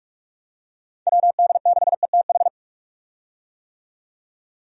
Morse code, heard as a keyed tone, about 47 wpm, 700 Hz. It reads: WD6ETH
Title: USA Call Signs - 6 Characters - 45wpm